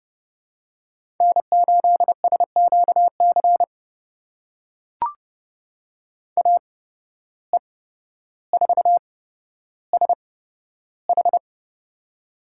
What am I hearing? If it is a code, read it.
Morse code, heard as a keyed tone, about 29 wpm, spaced as for 13 wpm, 700 Hz. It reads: N8SQC AE4SH
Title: USA Call Signs - 5 Characters - 8x Character Spacing - 30wpm